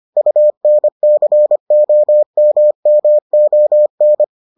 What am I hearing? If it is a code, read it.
Morse code, heard as a keyed tone, about 25 wpm, 600 Hz. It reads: UNCOMMON